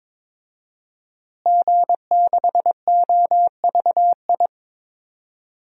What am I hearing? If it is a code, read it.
Morse code, heard as a keyed tone, about 22 wpm, 700 Hz. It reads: G6OVI
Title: DXCC-100 Call Signs - 5 Characters - Repeated 2x - 22wpm